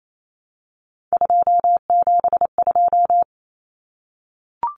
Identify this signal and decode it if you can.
Morse code, heard as a keyed tone, about 28 wpm, 700 Hz. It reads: 272
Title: Numbers - 3-Digit 28wpm